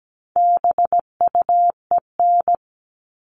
Morse code, heard as a keyed tone, about 17 wpm, 700 Hz. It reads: BUEN
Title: All Cut Numbers - 4-Digit 17wpm